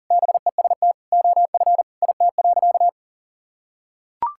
Morse code, heard as a keyed tone, about 40 wpm, 700 Hz. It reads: BEST OF IT.
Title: Sentences from Top 1000 Words - Review Part 5 - 40wpm